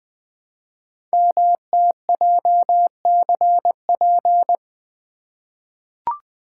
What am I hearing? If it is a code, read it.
Morse code, heard as a keyed tone, about 20 wpm, 700 Hz. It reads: MTJCP
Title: Code Groups - 5 Letters - 20wpm